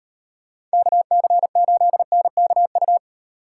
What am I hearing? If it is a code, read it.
Morse code, heard as a keyed tone, about 36 wpm, 700 Hz. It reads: KC8NKU